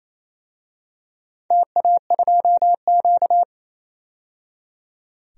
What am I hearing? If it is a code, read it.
Morse code, heard as a keyed tone, about 28 wpm, 700 Hz. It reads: TA2Q
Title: DXCC-100 Call Signs - 4 Characters - 28wpm